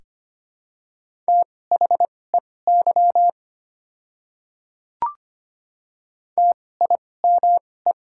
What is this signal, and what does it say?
Morse code, heard as a keyed tone, about 25 wpm, spaced as for 18 wpm, 700 Hz. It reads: THEY TIME
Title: 2x Character Spacing - Top 100 Words - 25wpm